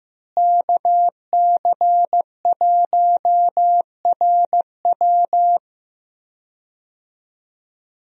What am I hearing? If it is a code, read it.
Morse code, heard as a keyed tone, about 15 wpm, 700 Hz. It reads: KC1RW
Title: USA Call Signs - 5 Characters - Repeated 2x - 15wpm